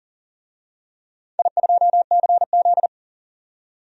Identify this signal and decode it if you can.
Morse code, heard as a keyed tone, about 43 wpm, 690 Hz. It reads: I2CZ
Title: DXCC-100 Call Signs - 4 Characters - 40wpm